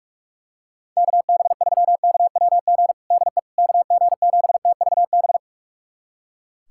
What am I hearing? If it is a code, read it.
Morse code, heard as a keyed tone, about 45 wpm, 700 Hz. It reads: KB3KWG DE KG7TVB
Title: USA Callsign DE Callsign 45wpm